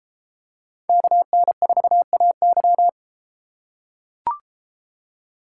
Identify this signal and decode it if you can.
Morse code, heard as a keyed tone, about 32 wpm, 700 Hz. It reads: KN4AY